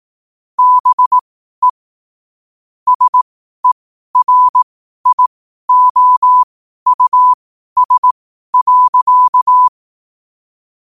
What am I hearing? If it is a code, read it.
Morse code, heard as a keyed tone, about 18 wpm, spaced as for 12 wpm, 1000 Hz. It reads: BE SERIOUS.
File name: qotdfiles / 12wpm